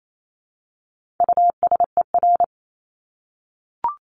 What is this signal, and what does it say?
Morse code, heard as a keyed tone, about 28 wpm, 700 Hz. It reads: USER
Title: Top 751-1000 Words 28wpm